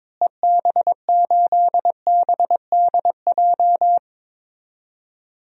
Morse code, heard as a keyed tone, about 21 wpm, 700 Hz. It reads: EB8BDJ
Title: DXCC-100 Call Signs - 6 Characters - Repeated 3x - 22wpm